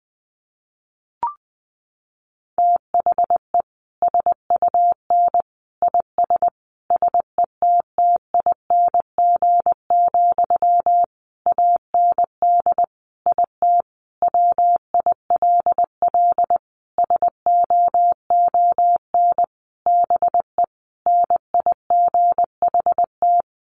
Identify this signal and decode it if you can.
Morse code, heard as a keyed tone, about 20 wpm, 700 Hz. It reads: THE SUN IS SETTING, AND IT WILL SOON BE NIGHT
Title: Sentences from Top 1300 Words 20wpm